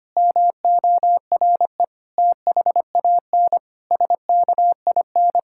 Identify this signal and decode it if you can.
Morse code, heard as a keyed tone, about 25 wpm, 700 Hz. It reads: MORE THAN SKIN
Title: Sentences from Top 1600 Words 25wpm